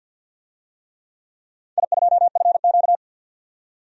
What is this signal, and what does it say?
Morse code, heard as a keyed tone, about 53 wpm, 700 Hz. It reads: I2FQ
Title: DXCC-100 Call Signs - 4 Characters - 50wpm